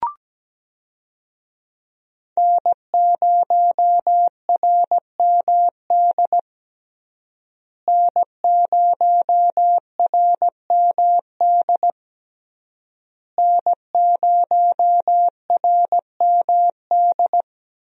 Morse code, heard as a keyed tone, about 17 wpm, 700 Hz. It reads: N0RMD N0RMD N0RMD
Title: USA Call Signs - 5 Characters - Repeated 3x - 17wpm